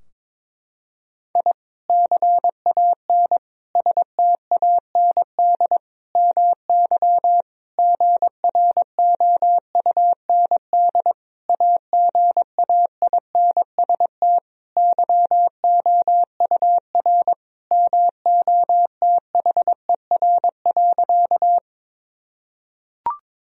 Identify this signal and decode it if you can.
Morse code, heard as a keyed tone, about 22 wpm, 700 Hz. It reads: I CAN STAND MY GROUND AGAINST YOUR MOTHER.
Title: Sentences from Top 1000 Words - Review Part 2 - 22wpm